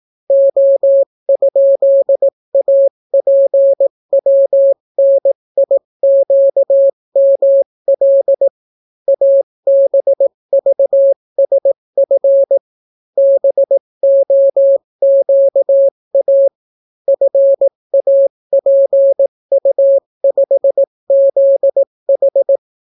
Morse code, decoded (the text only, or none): O?APWNIQML ABVSF BOQA FAPU5ZH